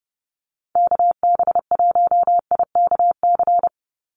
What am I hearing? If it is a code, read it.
Morse code, heard as a keyed tone, about 30 wpm, 700 Hz. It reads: KB1IKC
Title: USA Call Signs 30wpm